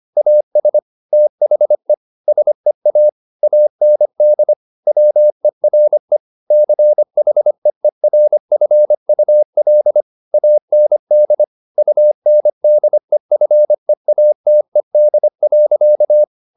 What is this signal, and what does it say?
Morse code, heard as a keyed tone, about 25 wpm, 600 Hz. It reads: AS THE SEA AND WERE CHEERFUL AND UNDEFEATED.
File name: the old man and the sea part 1 25wpm